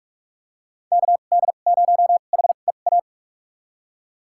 Morse code, heard as a keyed tone, about 46 wpm, spaced as for 33 wpm, 700 Hz. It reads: KD0HEA
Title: USA Call Signs - 6 Characters - 2x Character Spacing - 45wpm